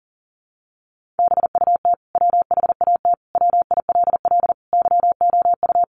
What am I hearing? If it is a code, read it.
Morse code, heard as a keyed tone, about 40 wpm, 700 Hz. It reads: BUT WHAT WILL YOU
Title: Sentences from Top 100 Words 40wpm